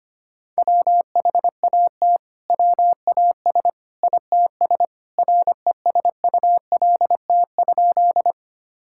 Morse code, heard as a keyed tone, about 25 wpm, 700 Hz. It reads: WHAT WAS ITS RESULT?
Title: Sentences from Top 1000 Words - Review Part 5 - 25wpm